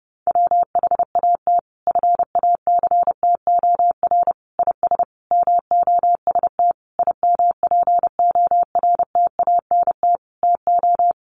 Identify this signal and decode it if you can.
Morse code, heard as a keyed tone, about 30 wpm, 700 Hz. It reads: WHAT FACTOR IS MOST IMPORTANT TO
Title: Sentences from Top 1300 Words 30wpm